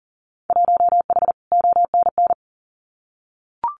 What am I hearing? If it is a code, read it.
Morse code, heard as a keyed tone, about 40 wpm, 700 Hz. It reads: 1H ONN
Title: Winter Field Day Exchanges 40wpm